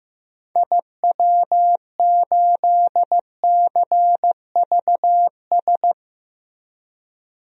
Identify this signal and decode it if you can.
Morse code, heard as a keyed tone, about 15 wpm, 700 Hz. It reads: IW8CVS
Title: DXCC-100 Call Signs - 6 Characters - Repeated 2x - 15wpm